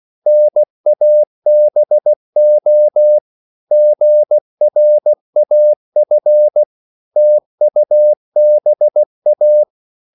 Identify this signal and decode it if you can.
Morse code, heard as a keyed tone, about 16 wpm, 600 Hz. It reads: NABO GRAF TUBA
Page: Korte danske ord 16wpm